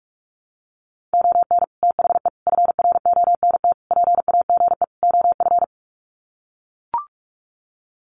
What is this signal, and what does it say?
Morse code, heard as a keyed tone, about 44 wpm, 700 Hz. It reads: ON THE FRONT PAGE OF
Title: Sets of 5 Words Encore 45wpm